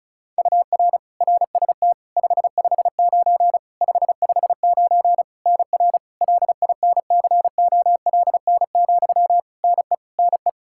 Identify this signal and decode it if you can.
Morse code, heard as a keyed tone, about 34 wpm, 700 Hz. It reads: UR RST 559 559 NR LINCOLN, NE NE
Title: Practice QSO - Exchange 1 - Beginning - 35wpm